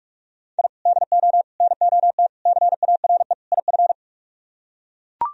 Morse code, heard as a keyed tone, about 44 wpm, 700 Hz. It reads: I DO NOT CARE IF